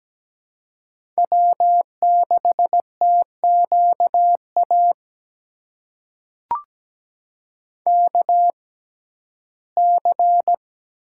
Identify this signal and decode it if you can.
Morse code, heard as a keyed tone, about 17 wpm, spaced as for 12 wpm, 700 Hz. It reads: W6TQA KC
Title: USA Call Signs - 5 Characters - 6x Character Spacing - 17wpm